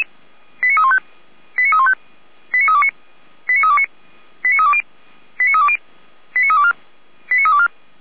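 Five-tone selective call (ZVEI), heard as quick runs of five tones, tones of about 70 ms. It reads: 89315 89315 89329 89329 89320 89320 89324 89324
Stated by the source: ZVEI five-tone selective call 89512 received on SDR.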